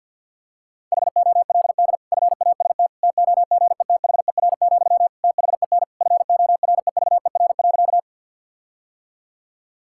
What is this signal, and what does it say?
Morse code, heard as a keyed tone, about 50 wpm, 700 Hz. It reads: HOLD FAST TOGETHER, THEN FOREVER.
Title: Sentences from Top 1600 Words 50wpm